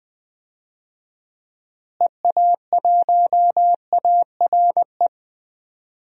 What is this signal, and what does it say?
Morse code, heard as a keyed tone, about 20 wpm, 700 Hz. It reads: EA1ARE
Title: DXCC-100 Call Signs - 6 Characters - Repeated 2x - 20wpm